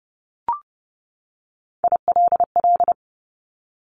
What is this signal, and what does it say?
Morse code, heard as a keyed tone, about 30 wpm, 700 Hz. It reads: ILL
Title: Top Three Letter Combinations in Words 30wpm